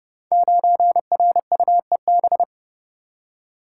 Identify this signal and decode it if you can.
Morse code, heard as a keyed tone, about 30 wpm, 700 Hz. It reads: 9RUEB
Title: Code Groups - 5 Letter:Numbers - 30wpm